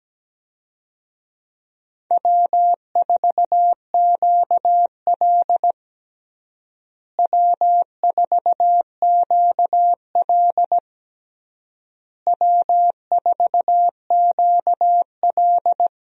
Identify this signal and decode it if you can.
Morse code, heard as a keyed tone, about 17 wpm, 700 Hz. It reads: W4QL W4QL W4QL
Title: USA Call Signs - 4 Characters - Repeated 3x - 17wpm